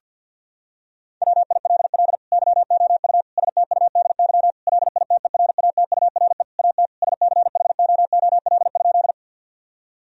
Morse code, heard as a keyed tone, about 50 wpm, 700 Hz. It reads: WILL YOU STUDY LITERATURE AT SCHOOL?
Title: Sentences from Top 1000 Words - Review Part 3 - 50wpm